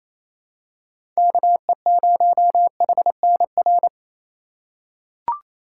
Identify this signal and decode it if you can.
Morse code, heard as a keyed tone, about 27 wpm, 700 Hz. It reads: KE0HNR
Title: USA Call Signs - 6 Characters - 2x Character Spacing - 28wpm